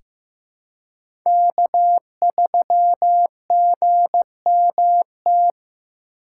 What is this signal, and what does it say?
Morse code, heard as a keyed tone, about 15 wpm, 700 Hz. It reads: K3GMT
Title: K1USN SST QSO Practice - 15wpm